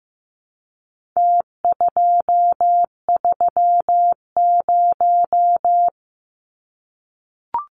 Morse code, heard as a keyed tone, about 15 wpm, 700 Hz. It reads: T230
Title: Numbers - Contest Style - 4-Digit 15wpm